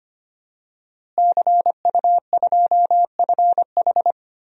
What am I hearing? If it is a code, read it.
Morse code, heard as a keyed tone, about 25 wpm, 700 Hz. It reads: CU2FH
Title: DXCC-100 Call Signs - 5 Characters - Repeated 3x - 25wpm